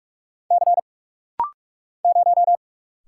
Morse code, heard as a keyed tone, about 45 wpm, 700 Hz. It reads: C 0
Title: Single Letter-Number - Rapid-Fire 45wpm